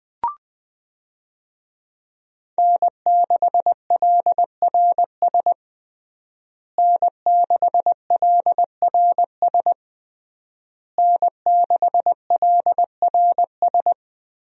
Morse code, heard as a keyed tone, about 20 wpm, 700 Hz. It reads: N6LRS N6LRS N6LRS
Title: USA Call Signs - 5 Characters - Repeated 3x - 20wpm